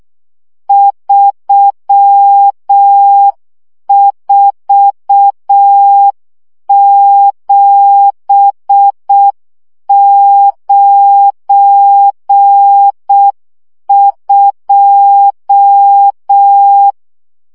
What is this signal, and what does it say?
Morse code, heard as a keyed tone, about 6 wpm, 800 Hz. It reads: 34792
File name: morze vizsga 2011-3 5